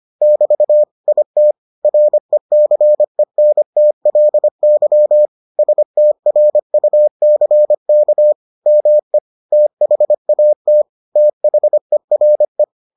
Morse code, heard as a keyed tone, about 25 wpm, 600 Hz. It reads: <BT> IT RECENTLY STRUCK ME THAT THERE